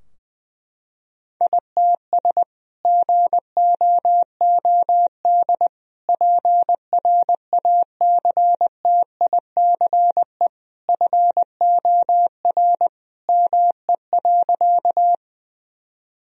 Morse code, heard as a keyed tone, about 19 wpm, 700 Hz. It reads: ITS GOOD PRACTICE FOR ME.
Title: Sentences from Top 1000 Words - Review Part 4 - 20wpm